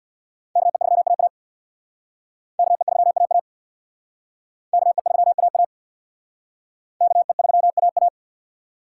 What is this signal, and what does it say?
Morse code, heard as a keyed tone, about 59 wpm, 700 Hz. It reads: KE3AA KE3AA KE3AA KE3AA
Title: Speed Racing - USA Call Signs 50wpm